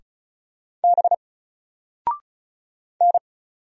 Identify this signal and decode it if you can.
Morse code, heard as a keyed tone, about 34 wpm, 700 Hz. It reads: B N
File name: Single-Letter-Number-35wpm